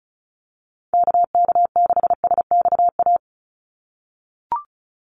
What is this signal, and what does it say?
Morse code, heard as a keyed tone, about 34 wpm, 700 Hz. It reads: KK6SXA
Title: USA Call Signs 35wpm